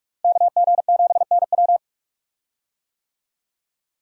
Morse code, heard as a keyed tone, about 41 wpm, 700 Hz. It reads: KG7NW
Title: USA Call Signs - 5 Characters - 45wpm